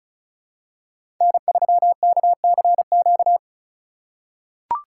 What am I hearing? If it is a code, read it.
Morse code, heard as a keyed tone, about 36 wpm, 700 Hz. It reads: N3KCQ
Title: USA Call Signs - 5 Characters - 6x Character Spacing - 35wpm